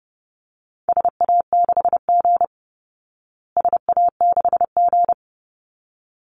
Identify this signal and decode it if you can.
Morse code, heard as a keyed tone, about 30 wpm, 700 Hz. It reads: SA6G SA6G
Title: Call Signs Repeated 3x - 30wpm